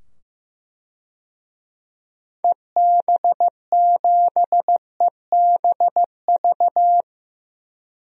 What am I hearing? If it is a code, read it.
Morse code, heard as a keyed tone, about 15 wpm, 700 Hz. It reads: EB7EBV
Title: DXCC-100 Call Signs - 6 Characters - 15wpm